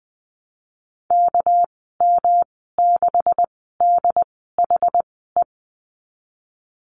Morse code, heard as a keyed tone, about 20 wpm, spaced as for 14 wpm, 700 Hz. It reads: KM6DHE
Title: 213 Lesson 34g - Number 8 - Call Signs 20wpm